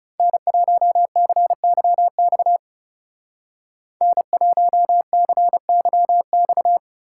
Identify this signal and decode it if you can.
Morse code, heard as a keyed tone, about 35 wpm, 700 Hz. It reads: N1CYX N1CYX
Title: Speed Racing - USA 1x3 Call Signs 30wpm